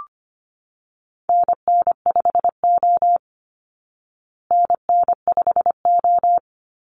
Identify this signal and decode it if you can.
Morse code, heard as a keyed tone, about 25 wpm, 700 Hz. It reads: NN5O NN5O
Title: Call Signs Repeated 3x - 25wpm